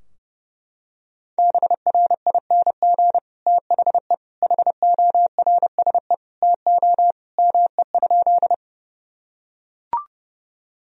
Morse code, heard as a keyed tone, about 32 wpm, 700 Hz. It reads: BRING THE HORSE TO ME?
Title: Sentences from Top 1000 Words - Review Part 3 - 30wpm